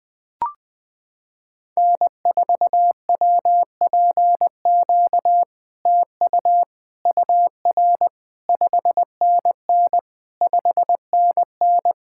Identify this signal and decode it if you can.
Morse code, heard as a keyed tone, about 20 wpm, 700 Hz. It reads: N4WPQ TU UR 5NN 5NN
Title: SOTA QSO Practice 20wpm